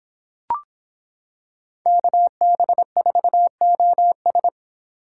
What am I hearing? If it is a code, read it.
Morse code, heard as a keyed tone, about 25 wpm, 700 Hz. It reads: KB4OS